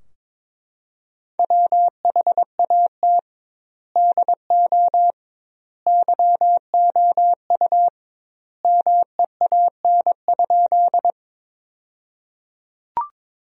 Morse code, heard as a keyed tone, about 22 wpm, 700 Hz. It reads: WHAT DO YOU MEAN?
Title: 2x Word-Spacing - Sentences from Top 200 Words - 22wpm